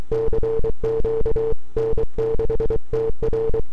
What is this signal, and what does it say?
Morse code, heard as a keyed tone, about 23 wpm, 450 Hz. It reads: CQ N6TR
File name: n6tr / 12Oct2009-160